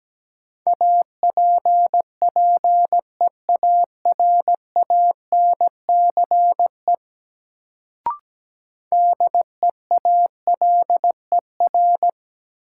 Morse code, heard as a keyed tone, about 17 wpm, 700 Hz. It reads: APPEARANCE DEALER
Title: Sentences from Top 1100 Words - New Word Drill - 17wpm